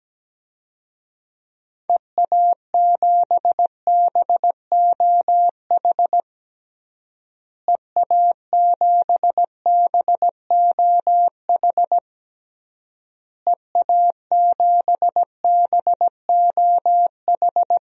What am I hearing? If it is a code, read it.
Morse code, heard as a keyed tone, about 17 wpm, 700 Hz. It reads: EA7BOH EA7BOH EA7BOH
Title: DXCC-100 Call Signs - 6 Characters - Repeated 3x - 17wpm